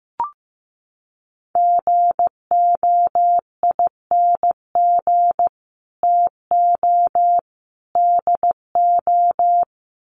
Morse code, heard as a keyed tone, about 15 wpm, 700 Hz. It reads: GOING TO DO
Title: Sets of 3 Words 15wpm